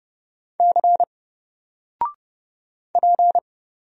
Morse code, heard as a keyed tone, about 30 wpm, 700 Hz. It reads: C P
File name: Single-Letter-Number-30wpm